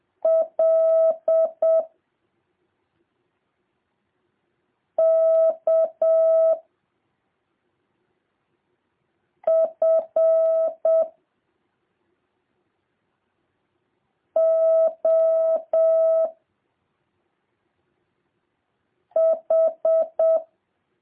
Morse code, heard as a keyed tone, about 7 wpm, 650 Hz. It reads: L K F O H